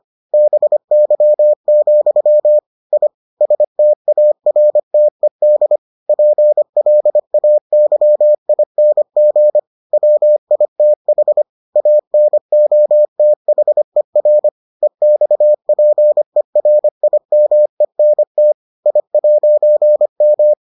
Morse code, heard as a keyed tone, about 25 wpm, 600 Hz. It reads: BY, I STARTED PLAYING WITH ANOTHER EXPERIMENT I'M